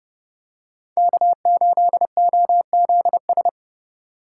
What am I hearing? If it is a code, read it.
Morse code, heard as a keyed tone, about 30 wpm, 700 Hz. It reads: K8OZS